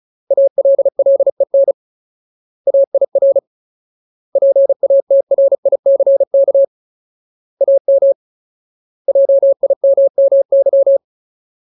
Morse code, heard as a keyed tone, about 34 wpm, 550 Hz. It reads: ALLEN AIR PATRICK AM JIMMY
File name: Ultimate Word List / 35WPM Ultimate